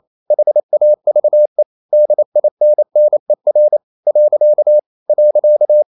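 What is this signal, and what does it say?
Morse code, heard as a keyed tone, about 28 wpm, 600 Hz. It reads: HAVE DINNER . .
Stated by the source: Morse code practice at 28 wpm, text of Ken Follett's novel Never in English, from CW med Gnister.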